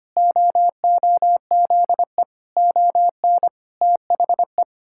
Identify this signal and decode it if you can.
Morse code, heard as a keyed tone, about 25 wpm, 700 Hz. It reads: OOZE ON THE
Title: Pangram Challenge v2 - 25wpm